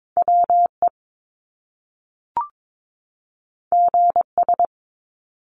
Morse code, heard as a keyed tone, about 22 wpm, 700 Hz. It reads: WE GS
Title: Top Two Letter Combinations in Words 22wpm